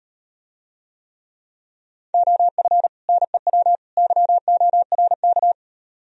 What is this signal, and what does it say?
Morse code, heard as a keyed tone, about 38 wpm, 700 Hz. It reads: OF NEW YORK